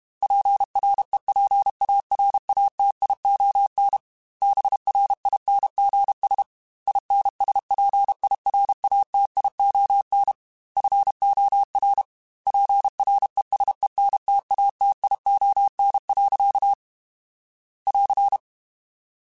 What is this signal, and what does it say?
Morse code, heard as a keyed tone, about 32 wpm, 800 Hz. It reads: PREPARATION BRINGS INSPIRATION FOR PRESENTATION. <AR>
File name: Quotes / Quotes-mp3s / Persperation @30